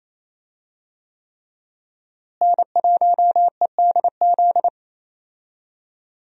Morse code, heard as a keyed tone, about 28 wpm, 700 Hz. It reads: N1EDZ
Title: USA Call Signs - 5 Characters - Repeated 3x - 28wpm